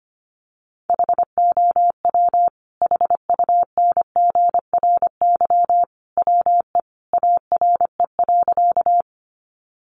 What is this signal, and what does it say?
Morse code, heard as a keyed tone, about 25 wpm, 700 Hz. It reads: HOW HUNGRY WE ARE.
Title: Sentences from Top 1400 Words 25wpm